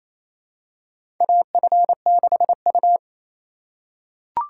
Morse code, heard as a keyed tone, about 28 wpm, 700 Hz. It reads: AF6U
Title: USA Call Signs - 4 Characters - 8x Character Spacing - 28wpm